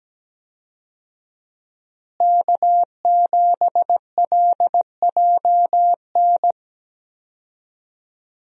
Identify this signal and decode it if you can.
Morse code, heard as a keyed tone, about 17 wpm, 700 Hz. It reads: K7LJN
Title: USA Call Signs - 5 Characters - Repeated 2x - 17wpm